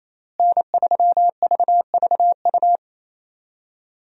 Morse code, heard as a keyed tone, about 28 wpm, 700 Hz. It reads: N3VVU
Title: USA Call Signs - 5 Characters - Repeated 3x - 28wpm